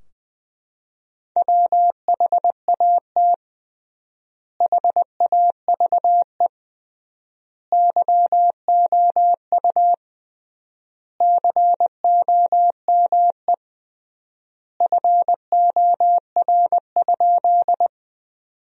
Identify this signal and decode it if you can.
Morse code, heard as a keyed tone, about 20 wpm, 700 Hz. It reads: WHAT HAVE YOU COME FOR?
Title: 3x Word-Spacing - Sentences from Top 100 Words - 20wpm